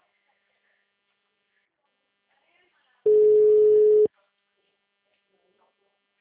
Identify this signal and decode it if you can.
Ringback tone (Europe, 425 Hz), with speech.
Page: Atibaia - SP